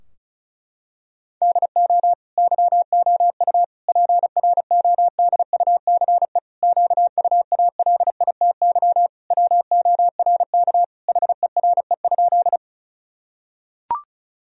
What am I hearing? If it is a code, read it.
Morse code, heard as a keyed tone, about 34 wpm, 700 Hz. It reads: DO YOU PRODUCE QUALITY WORK HERE?
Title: Sentences from Top 1000 Words - Review Part 4 - 35wpm